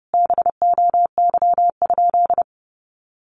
Morse code, heard as a keyed tone, about 30 wpm, 700 Hz. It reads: BOY?
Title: Sentences from Top 1100 Words 30wpm